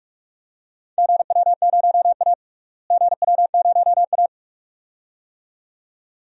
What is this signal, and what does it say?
Morse code, heard as a keyed tone, about 46 wpm, 700 Hz. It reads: GW0A GW0A
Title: DXCC-100 Call Signs - 4 Characters - Repeated 2x - 45wpm